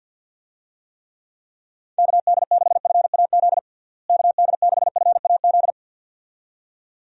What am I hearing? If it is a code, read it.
Morse code, heard as a keyed tone, about 51 wpm, 700 Hz. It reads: KD6FAZ KD6FAZ
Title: USA Call Signs - 6 Characters - Repeated 2x - 50wpm